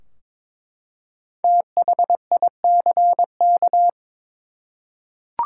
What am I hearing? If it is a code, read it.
Morse code, heard as a keyed tone, about 23 wpm, 700 Hz. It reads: THICK
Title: Five Letter Words - 22wpm